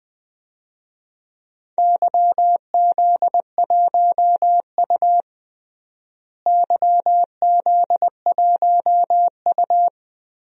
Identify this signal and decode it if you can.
Morse code, heard as a keyed tone, about 20 wpm, 700 Hz. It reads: YZ1U YZ1U
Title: DXCC-100 Call Signs - 4 Characters - Repeated 2x - 20wpm